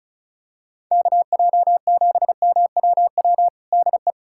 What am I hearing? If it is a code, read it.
Morse code, heard as a keyed tone, about 36 wpm, 700 Hz. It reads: KJ7MWW DE
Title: USA Callsign DE Callsign - 6 Characters - 35wpm